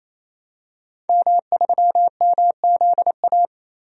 Morse code, heard as a keyed tone, about 28 wpm, 700 Hz. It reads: M3MZA
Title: DXCC-100 Call Signs - 5 Characters - 28wpm